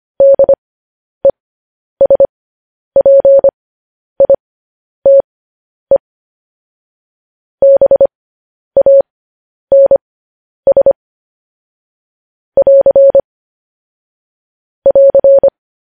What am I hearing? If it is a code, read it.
Morse code, heard as a keyed tone, about 25 wpm, spaced as for 10 wpm, 550 Hz. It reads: DESPITE BANS <AR> <AR>
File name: podcast / News headlines-1x-25x10